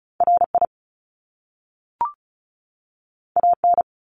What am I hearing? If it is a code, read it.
Morse code, heard as a keyed tone, about 36 wpm, 700 Hz. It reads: RI AN
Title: Top Two Letter Combinations in Words 35wpm